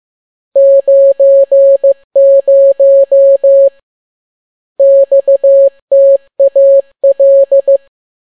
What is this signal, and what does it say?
Morse code, heard as a keyed tone, about 15 wpm, 550 Hz. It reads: 90 XTAL
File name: n6tt p1 15C13